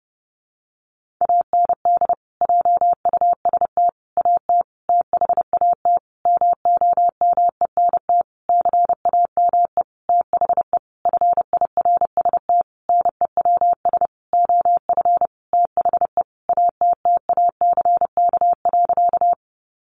Morse code, heard as a keyed tone, about 28 wpm, 700 Hz. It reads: AND JUST AT THAT MOMENT CAME THE FIRST NEWS OF THE ATTACK.
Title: Sentences from Top 1500 Words 30wpm